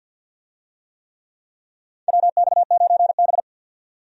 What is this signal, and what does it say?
Morse code, heard as a keyed tone, about 52 wpm, 700 Hz. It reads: WX9B
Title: USA Call Signs - 4 Characters - 50wpm